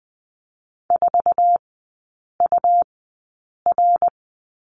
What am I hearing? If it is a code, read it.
Morse code, heard as a keyed tone, about 19 wpm, spaced as for 8 wpm, 700 Hz. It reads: 4UR
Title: 212 Lesson 34e - Number 8 - Three Characters 20wpm